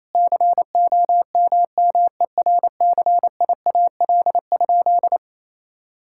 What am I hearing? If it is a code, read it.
Morse code, heard as a keyed tone, about 28 wpm, 700 Hz. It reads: COMMERCIAL?
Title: Sentences from Top 1000 Words - Review Part 5 - 28wpm